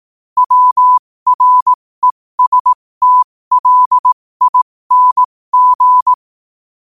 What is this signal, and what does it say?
Morse code, heard as a keyed tone, about 18 wpm, spaced as for 15 wpm, 1000 Hz. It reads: WRESTLING
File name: qotdfiles / 15wpm